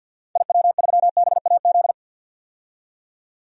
Morse code, heard as a keyed tone, about 52 wpm, 700 Hz. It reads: IW3BAZ
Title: DXCC-100 Call Signs - 6 Characters - 50wpm